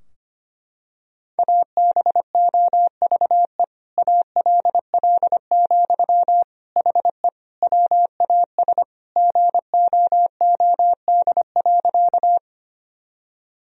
Morse code, heard as a keyed tone, about 25 wpm, 700 Hz. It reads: ABOVE ALL, HE WAS GOOD.
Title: Sentences from Top 1000 Words - Review Part 2 - 25wpm